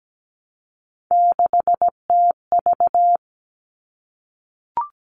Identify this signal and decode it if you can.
Morse code, heard as a keyed tone, about 17 wpm, 700 Hz. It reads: 6TV
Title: All Cut Numbers - 3-Digit 17wpm